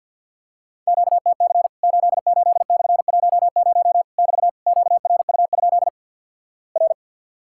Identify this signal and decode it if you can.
Morse code, heard as a keyed tone, about 50 wpm, 700 Hz. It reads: QTC 88/10 <BT> QRU? R
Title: Worked All Europe DX Contest - QTC Exchange with Repeats - 50wpm